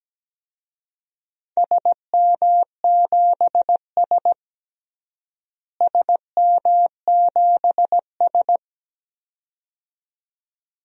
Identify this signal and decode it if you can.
Morse code, heard as a keyed tone, about 17 wpm, 700 Hz. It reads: SM7S SM7S
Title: DXCC-100 Call Signs - 4 Characters - Repeated 2x - 17wpm